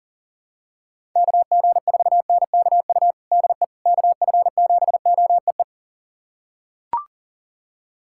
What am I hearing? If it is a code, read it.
Morse code, heard as a keyed tone, about 40 wpm, 700 Hz. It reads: KG4NKU DE KF7OEE
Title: USA Callsign DE Callsign - 6 Characters - 40wpm